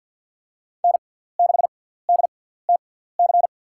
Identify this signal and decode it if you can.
Morse code, heard as a keyed tone, about 48 wpm, spaced as for 17 wpm, 700 Hz. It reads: N6DTC
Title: USA Call Signs - 5 Characters - 6x Character Spacing - 50wpm